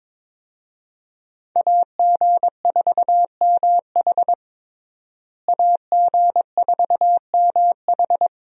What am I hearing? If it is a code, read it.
Morse code, heard as a keyed tone, about 22 wpm, 700 Hz. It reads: AG4MH AG4MH
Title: USA Call Signs - 5 Characters - Repeated 3x - 22wpm